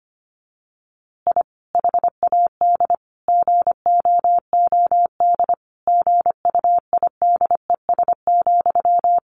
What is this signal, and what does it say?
Morse code, heard as a keyed tone, about 25 wpm, 700 Hz. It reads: I HAD GOOD GUIDES,
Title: A Sound Of Thunder 25wpm